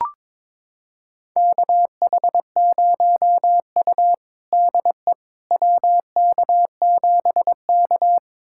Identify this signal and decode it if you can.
Morse code, heard as a keyed tone, about 22 wpm, 700 Hz. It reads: KH0U DE WK7K
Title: USA Callsign DE Callsign - 4 Characters - 22wpm